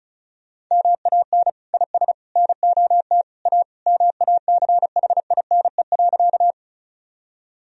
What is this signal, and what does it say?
Morse code, heard as a keyed tone, about 36 wpm, 700 Hz. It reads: MAN IS NOT A MACHINE.
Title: Sentences from Top 1000 Words - Review Part 2 - 35wpm